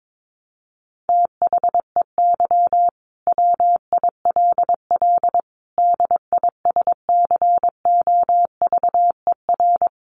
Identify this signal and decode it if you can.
Morse code, heard as a keyed tone, about 22 wpm, 700 Hz. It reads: THEY WILL DISCOVER
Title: Sentences from Top 1100 Words 22wpm